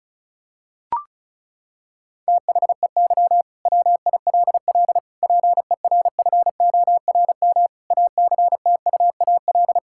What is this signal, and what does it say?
Morse code, heard as a keyed tone, about 34 wpm, 700 Hz. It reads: THEY WILL PERFORM ACTUAL
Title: Sentences from Top 1000 Words - Review Part 5 - 35wpm